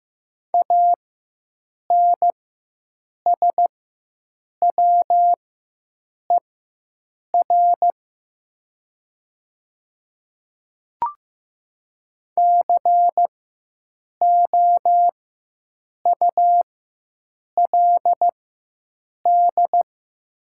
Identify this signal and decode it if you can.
Morse code, heard as a keyed tone, about 15 wpm, spaced as for 7 wpm, 690 Hz. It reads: ANSWER COULD
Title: 4x Character Spacing - Top 500 Words - 15wpm